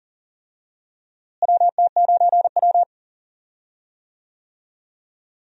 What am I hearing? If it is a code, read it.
Morse code, heard as a keyed tone, about 37 wpm, 700 Hz. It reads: WT9W
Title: USA Call Signs - 4 Characters - Repeated 3x - 40wpm